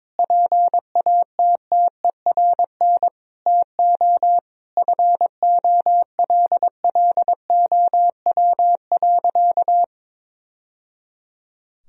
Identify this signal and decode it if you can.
Morse code, heard as a keyed tone, about 22 wpm, 700 Hz. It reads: PATTERN TO FOLLOW.
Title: Sentences from Top 1000 Words - Review Part 2 - 22wpm